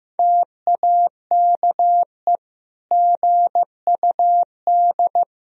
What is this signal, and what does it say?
Morse code, heard as a keyed tone, about 15 wpm, 700 Hz. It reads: TAKE GUD
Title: Sentences from Top 400 Words and Abbreviations 15wpm